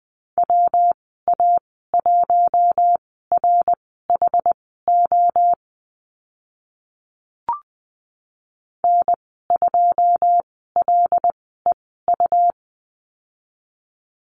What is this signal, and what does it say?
Morse code, heard as a keyed tone, about 20 wpm, spaced as for 14 wpm, 700 Hz. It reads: WA1RHO N2LEU
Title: 128 Lesson 21g - Number 6 - Call Signs 20wpm